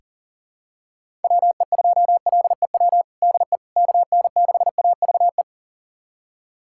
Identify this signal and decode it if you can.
Morse code, heard as a keyed tone, about 40 wpm, 700 Hz. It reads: WE2LEW DE KN6AVE
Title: USA Callsign DE Callsign - 6 Characters - 40wpm